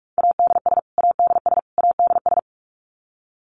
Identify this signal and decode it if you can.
Morse code, heard as a keyed tone, about 44 wpm, 700 Hz. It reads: ADS ADS ADS
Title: Abbreviations Repeated 3x - 45wpm